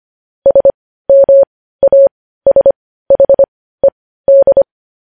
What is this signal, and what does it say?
Morse code, heard as a keyed tone, about 25 wpm, spaced as for 15 wpm, 550 Hz. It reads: SMASHED
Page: news headlines in morse code at 15 wpm